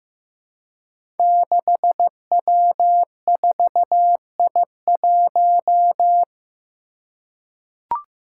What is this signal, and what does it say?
Morse code, heard as a keyed tone, about 15 wpm, 700 Hz. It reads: 6W4I1
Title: Five Challenging Letter-Number Combinations 15wpm